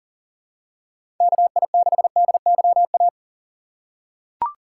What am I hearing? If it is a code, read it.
Morse code, heard as a keyed tone, about 40 wpm, 700 Hz. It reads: KI6DYA
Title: USA Call Signs - 6 Characters - 3x Character Spacing - 40wpm